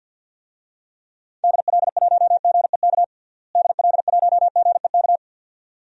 Morse code, heard as a keyed tone, about 53 wpm, 700 Hz. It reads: DL1GEK DL1GEK
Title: DXCC-100 Call Signs - 6 Characters - Repeated 2x - 50wpm